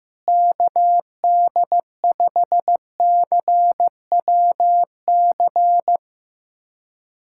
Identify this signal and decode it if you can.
Morse code, heard as a keyed tone, about 15 wpm, 700 Hz. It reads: KD5CWC
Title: USA Callsign DE Callsign 15wpm